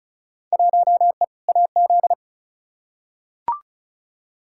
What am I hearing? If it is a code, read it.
Morse code, heard as a keyed tone, about 34 wpm, 700 Hz. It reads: 1E AZ
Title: Field Day Exchanges - 35wpm